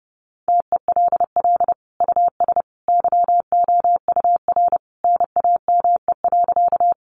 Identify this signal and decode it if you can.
Morse code, heard as a keyed tone, about 30 wpm, 700 Hz. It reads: TELL US YOUR NAME.
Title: Sentences from Top 200 Words 30wpm